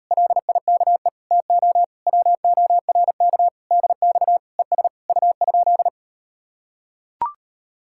Morse code, heard as a keyed tone, about 38 wpm, 700 Hz. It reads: LIKE TO WORK DX ES U?